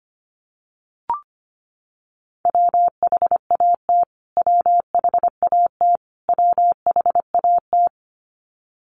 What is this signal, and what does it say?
Morse code, heard as a keyed tone, about 25 wpm, 700 Hz. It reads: WHAT WHAT WHAT
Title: Top 500 Words Repeated 25wpm